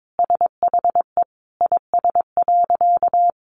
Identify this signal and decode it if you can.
Morse code, heard as a keyed tone, about 22 wpm, 700 Hz. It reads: SHE IS.
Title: Sentences from Top 900 Words 22wpm